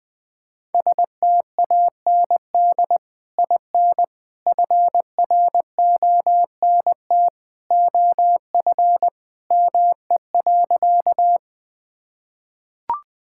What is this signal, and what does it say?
Morse code, heard as a keyed tone, about 20 wpm, 700 Hz. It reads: STAND IN FRONT OF ME.
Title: Sentences with 2-Word Phrases - 20wpm